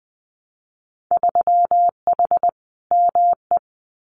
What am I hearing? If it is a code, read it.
Morse code, heard as a keyed tone, about 19 wpm, 700 Hz. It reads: 3H ME
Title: Winter Field Day Exchanges Encore 20wpm